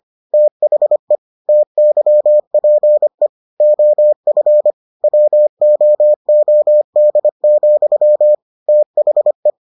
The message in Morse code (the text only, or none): THE TYPE OF WOOD, THE